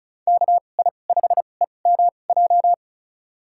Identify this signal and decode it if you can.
Morse code, heard as a keyed tone, about 34 wpm, spaced as for 25 wpm, 700 Hz. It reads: KI5EMJ
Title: USA Call Signs - 6 Characters - 2x Character Spacing - 35wpm